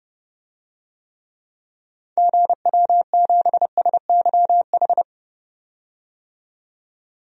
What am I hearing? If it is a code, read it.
Morse code, heard as a keyed tone, about 30 wpm, 700 Hz. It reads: GW7SYH
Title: DXCC-100 Call Signs - 6 Characters - 30wpm